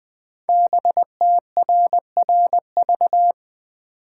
Morse code, heard as a keyed tone, about 20 wpm, 700 Hz. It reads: BTRRV
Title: Code Groups - 5 Letters - 20wpm